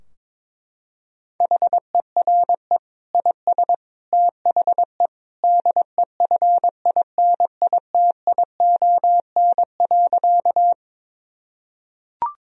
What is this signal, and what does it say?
Morse code, heard as a keyed tone, about 22 wpm, 700 Hz. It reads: HERE IS THE DEFINITION.
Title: Sentences from Top 1000 Words - Review Part 4 - 22wpm